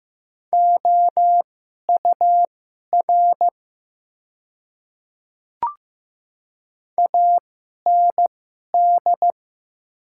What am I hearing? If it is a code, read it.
Morse code, heard as a keyed tone, about 15 wpm, spaced as for 11 wpm, 690 Hz. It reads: OUR AND
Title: 2x Character Spacing - Top 300 Words - 15wpm